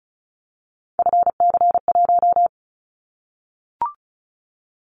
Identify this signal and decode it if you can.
Morse code, heard as a keyed tone, about 34 wpm, 700 Hz. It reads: FC1
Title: Callsign Prefixes for Top 25 DXCC - 35wpm